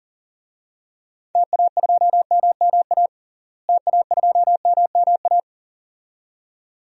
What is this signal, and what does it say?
Morse code, heard as a keyed tone, about 39 wpm, 700 Hz. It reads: TA2MMA TA2MMA
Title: DXCC-100 Call Signs - 6 Characters - Repeated 2x - 40wpm